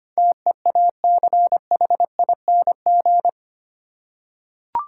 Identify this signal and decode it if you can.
Morse code, heard as a keyed tone, about 25 wpm, 700 Hz. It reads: TEACHING
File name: Eight Letter Words - 25wpm